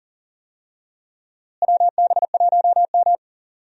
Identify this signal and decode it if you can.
Morse code, heard as a keyed tone, about 38 wpm, 700 Hz. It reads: WB1M
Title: USA Call Signs - 4 Characters - Repeated 2x - 40wpm